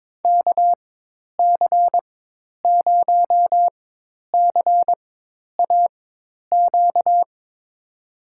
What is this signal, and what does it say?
Morse code, heard as a keyed tone, about 22 wpm, spaced as for 10 wpm, 700 Hz. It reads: KC0CAQ
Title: USA Call Signs - 6 Characters - 4x Character Spacing - 22wpm